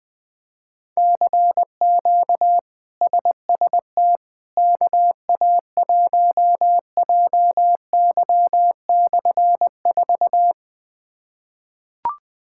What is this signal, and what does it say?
Morse code, heard as a keyed tone, about 20 wpm, 700 Hz. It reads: CQ SST KA1JY/4
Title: K1USN SST QSO Practice - 20wpm